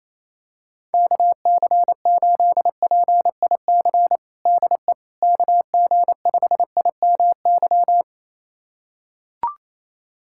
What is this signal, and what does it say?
Morse code, heard as a keyed tone, about 28 wpm, 700 Hz. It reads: KC8PIC DE KG5IMY
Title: USA Callsign DE Callsign - 6 Characters - 28wpm